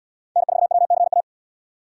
Morse code, heard as a keyed tone, about 77 wpm, 700 Hz. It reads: N4WCA